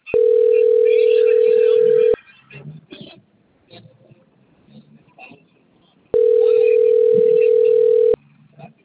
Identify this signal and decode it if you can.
Ringback tone (North America), 440 + 480 Hz, with speech.
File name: out- - - - - .59559